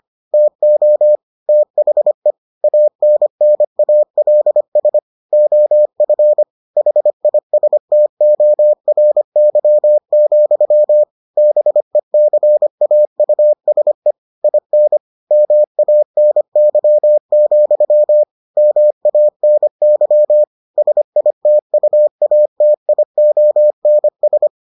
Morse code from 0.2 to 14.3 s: TO THE ANNALS OF HISTORY, BECAUSE

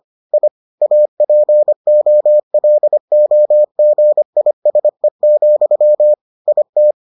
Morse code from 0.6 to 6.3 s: APOLOGISE,

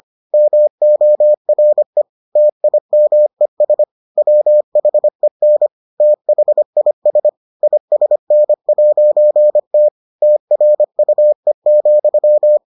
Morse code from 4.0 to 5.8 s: WHEN

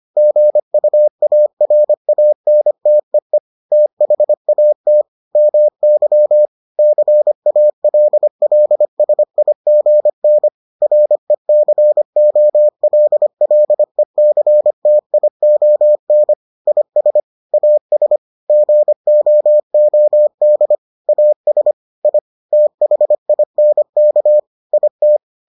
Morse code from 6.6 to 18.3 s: CALLSIGN RECOLLECTION IS AS

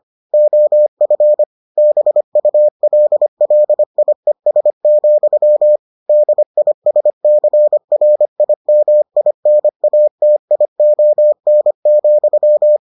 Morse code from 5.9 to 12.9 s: DISCRIMINATION,

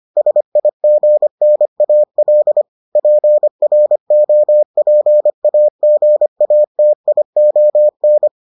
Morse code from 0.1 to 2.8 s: SIGNAL

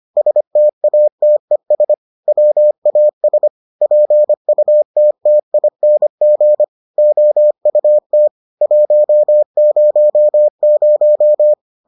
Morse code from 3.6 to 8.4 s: PUTTING OUT

